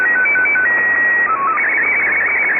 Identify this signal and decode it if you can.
SSTV (slow-scan television), Scottie 1.
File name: AmphetamineRadio6925Usb